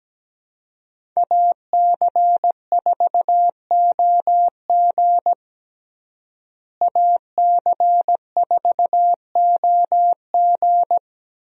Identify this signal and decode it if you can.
Morse code, heard as a keyed tone, about 17 wpm, 700 Hz. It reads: AC4OG AC4OG
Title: USA Call Signs - 5 Characters - Repeated 3x - 17wpm